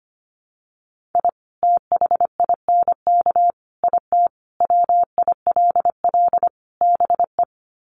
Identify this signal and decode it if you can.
Morse code, heard as a keyed tone, about 25 wpm, 700 Hz. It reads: I THINK IT WILL BE
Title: Sets of 5 Words Encore2 25wpm